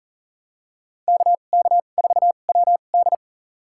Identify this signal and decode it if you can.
Morse code, heard as a keyed tone, about 40 wpm, spaced as for 29 wpm, 700 Hz. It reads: KK4WD
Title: USA Call Signs - 5 Characters - 2x Character Spacing - 40wpm